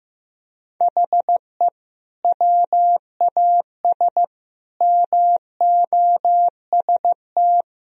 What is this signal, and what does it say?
Morse code, heard as a keyed tone, about 15 wpm, 700 Hz. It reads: HE WAS MOST
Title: Sentences from Top 1000 Words - Review Part 4 - 15wpm